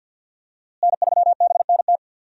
Morse code, heard as a keyed tone, about 52 wpm, 700 Hz. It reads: N3BNT
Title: USA Call Signs - 5 Characters - Repeated 2x - 50wpm